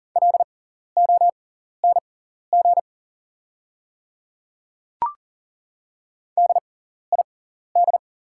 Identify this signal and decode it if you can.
Morse code, heard as a keyed tone, about 42 wpm, spaced as for 14 wpm, 690 Hz. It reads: LONG DID
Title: 6x Character Spacing - Top 100 Words - 40wpm